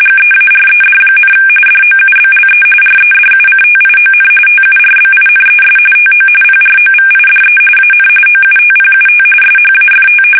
RTTY, about 45.45 baud. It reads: GEARED, DOUBLE REDUCTION TURBINES TO PROVIDE AN OUTPUT SHAFT
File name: fsk850 / battleshipnj